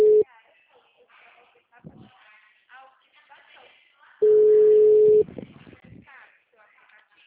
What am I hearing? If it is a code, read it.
Ringback tone (Europe, 425 Hz), with speech.